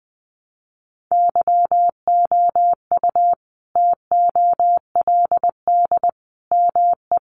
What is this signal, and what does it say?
Morse code, heard as a keyed tone, about 20 wpm, 700 Hz. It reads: YOU TOLD ME
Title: Sentences from Top 1000 Words 20wpm